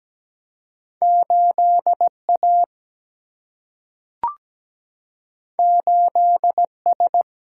Morse code, heard as a keyed tone, about 17 wpm, 700 Hz. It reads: 8 A 8 S
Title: Single-Letter Single-Number 17wpm